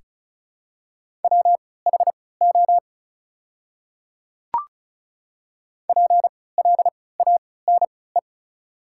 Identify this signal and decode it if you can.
Morse code, heard as a keyed tone, about 34 wpm, spaced as for 20 wpm, 700 Hz. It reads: WHO PLANE
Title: 3x Character Spacing - Top 500 Words - 35wpm